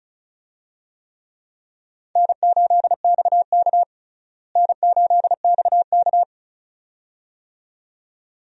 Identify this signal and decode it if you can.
Morse code, heard as a keyed tone, about 34 wpm, 700 Hz. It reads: N8XK N8XK
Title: USA Call Signs - 4 Characters - Repeated 2x - 35wpm